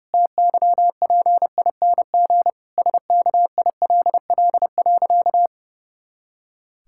Morse code, heard as a keyed tone, about 32 wpm, 700 Hz. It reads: TYPING SKILL.
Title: Pangram Challenge v2 - 30wpm